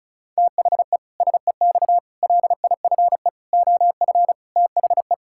Morse code, heard as a keyed tone, about 36 wpm, 700 Hz. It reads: THE SEX LIFE OF THE
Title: Pangram Challenge v2 - 35wpm